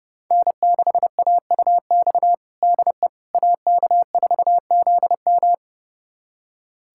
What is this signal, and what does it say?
Morse code, heard as a keyed tone, about 30 wpm, 700 Hz. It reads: N6AUX DE AK4ZM
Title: USA Callsign DE Callsign - 5 Characters - 30wpm